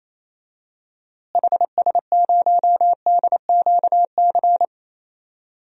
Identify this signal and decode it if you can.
Morse code, heard as a keyed tone, about 28 wpm, 700 Hz. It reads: HS0DQC
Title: DXCC-100 Call Signs - 6 Characters - Repeated 2x - 28wpm